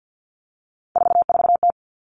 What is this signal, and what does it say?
Morse code, heard as a keyed tone, about 50 wpm, 700 Hz. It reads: 44T
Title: Three Challenging Letter-Number Combinations 50wpm